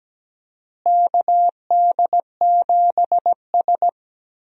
Morse code, heard as a keyed tone, about 17 wpm, 700 Hz. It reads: KD7S
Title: USA Call Signs - 4 Characters - Repeated 3x - 17wpm